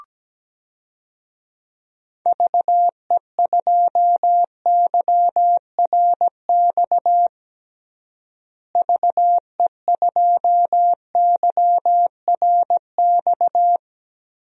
Morse code, heard as a keyed tone, about 17 wpm, 700 Hz. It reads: VE2YRX VE2YRX
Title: DXCC-100 Call Signs - 6 Characters - Repeated 3x - 17wpm